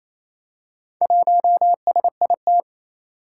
Morse code, heard as a keyed tone, about 29 wpm, 700 Hz. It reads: 1SIT
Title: Four Challenging Letter-Number Combinations 28wpm